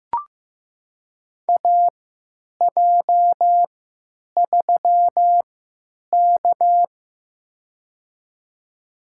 Morse code, heard as a keyed tone, about 15 wpm, spaced as for 9 wpm, 700 Hz. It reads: AJ3K
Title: USA Call Signs - 4 Characters - 3x Character Spacing - 15wpm